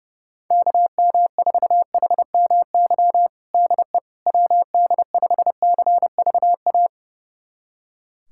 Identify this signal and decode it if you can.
Morse code, heard as a keyed tone, about 30 wpm, 700 Hz. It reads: KM4HMY DE WD5CVA
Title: USA Callsign DE Callsign - 6 Characters - 30wpm